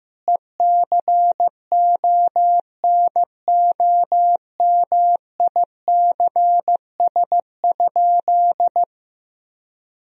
Morse code, heard as a keyed tone, about 15 wpm, 700 Hz. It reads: ECONOMICS?
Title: Sentences from Top 1000 Words - Review Part 3 - 15wpm